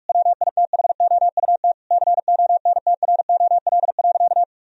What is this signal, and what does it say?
Morse code, heard as a keyed tone, about 43 wpm, 700 Hz. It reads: WITHOUT CONTROL.
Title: Sentences from Top 1000 Words - Review Part 3 - 45wpm